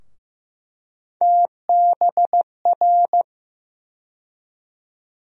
Morse code, heard as a keyed tone, about 15 wpm, 700 Hz. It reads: TBR
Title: USA Call Sign Postfixes - 3 Letters 15wpm